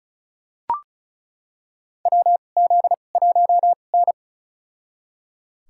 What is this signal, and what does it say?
Morse code, heard as a keyed tone, about 36 wpm, spaced as for 25 wpm, 700 Hz. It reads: WZ1N
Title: USA Call Signs - 2x Character Spacing - 35wpm